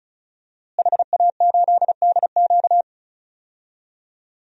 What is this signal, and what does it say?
Morse code, heard as a keyed tone, about 34 wpm, 700 Hz. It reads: HA8DQ
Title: DXCC-100 Call Signs - 5 Characters - Repeated 3x - 35wpm